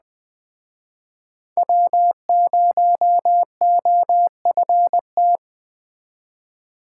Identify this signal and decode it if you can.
Morse code, heard as a keyed tone, about 21 wpm, 700 Hz. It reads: W0OFT